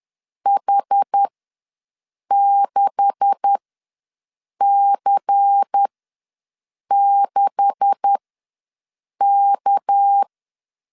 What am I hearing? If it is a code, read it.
Morse code, heard as a keyed tone, about 10 wpm, spaced as for 6 wpm, 800 Hz. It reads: H6C6K
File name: LEZIONI / 9 286 ws 10wpm 4032